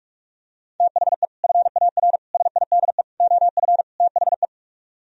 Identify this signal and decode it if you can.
Morse code, heard as a keyed tone, about 45 wpm, 700 Hz. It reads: THE FAR SIDE OF THE